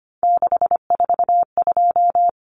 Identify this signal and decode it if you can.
Morse code, heard as a keyed tone, about 25 wpm, 690 Hz. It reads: 642
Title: Numbers - 3-Digit 25wpm